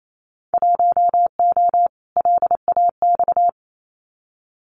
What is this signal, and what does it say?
Morse code, heard as a keyed tone, about 28 wpm, 700 Hz. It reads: 1O LAX
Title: Winter Field Day Exchanges 28wpm